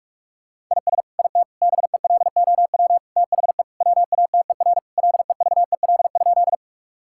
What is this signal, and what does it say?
Morse code, heard as a keyed tone, about 46 wpm, 700 Hz. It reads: IS IT BELOW THE WATER LEVEL?
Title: Sentences from Top 1000 Words - Review Part 5 - 45wpm